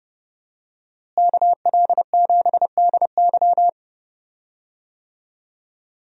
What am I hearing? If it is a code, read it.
Morse code, heard as a keyed tone, about 30 wpm, 700 Hz. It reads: KL7DY
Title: DXCC-100 Call Signs - 5 Characters - Repeated 2x - 30wpm